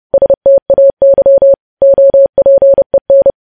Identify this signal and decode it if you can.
Morse code, heard as a keyed tone, about 30 wpm, 550 Hz. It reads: STAY OPEN
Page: News Headlines in Morse Code at 30 WPM